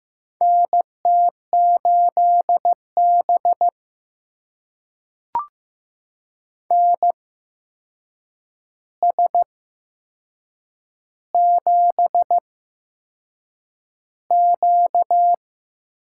Morse code, heard as a keyed tone, about 15 wpm, spaced as for 6 wpm, 700 Hz. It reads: NT8B NS7Q
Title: USA Call Signs - 4 Characters - 8x Character Spacing - 15wpm